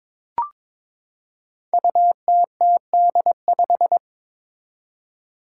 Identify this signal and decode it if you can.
Morse code, heard as a keyed tone, about 22 wpm, 700 Hz. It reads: UTTD5
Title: Five Challenging Letter-Number Combinations 22wpm